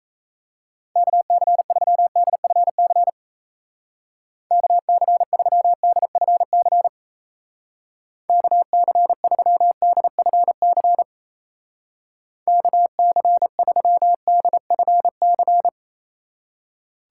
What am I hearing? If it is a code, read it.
Morse code, heard as a keyed tone, about 35 wpm, 700 Hz. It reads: KC3DFC KC3DFC KC3DFC KC3DFC
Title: Speed Racing - USA 2x3 Call Signs 28wpm